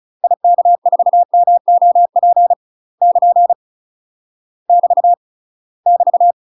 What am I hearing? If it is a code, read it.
Morse code, heard as a keyed tone, about 34 wpm, 700 Hz. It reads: IK4MOP <KN> <BT> <BT>
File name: UK MASTER QSOs TRAINING/35WPM